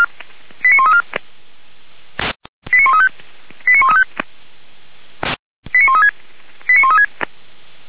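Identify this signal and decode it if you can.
Five-tone selective call (ZVEI), heard as quick runs of five tones, tones of about 70 ms. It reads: 89134 89135 89135 89136 89136
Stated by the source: ZVEI five-tone selective call 89133 received via SDR, logged on the Feuerwehr Orlamünde fire brigade audio server.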